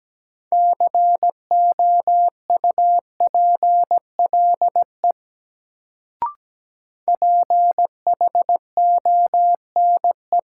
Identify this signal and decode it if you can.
Morse code, heard as a keyed tone, about 17 wpm, 700 Hz. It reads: COUPLE PHONE
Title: Sentences from Top 1100 Words - New Word Drill Alt - 17wpm